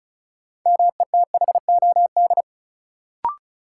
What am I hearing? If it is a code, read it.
Morse code, heard as a keyed tone, about 36 wpm, 700 Hz. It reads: METHOD
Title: Sentences from Top 600 Words - New Word Drill - 35wpm